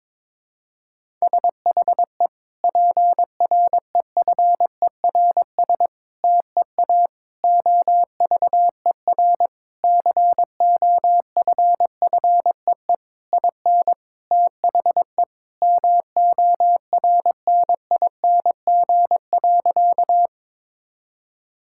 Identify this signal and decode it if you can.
Morse code, heard as a keyed tone, about 22 wpm, 700 Hz. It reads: SHE PREFERS TEA OVER COFFEE IN THE MORNING.
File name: Difficult Sentences with Hints - 22wpm